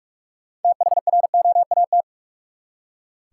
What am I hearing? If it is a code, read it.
Morse code, heard as a keyed tone, about 46 wpm, 700 Hz. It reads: THROAT
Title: Six Letter Words - 45wpm